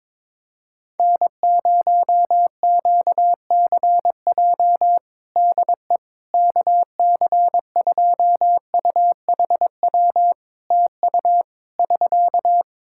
Morse code, heard as a keyed tone, about 21 wpm, 700 Hz. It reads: N0QCJ DE KC2UHW TU <SK>
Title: K1USN SST QSO Practice 22wpm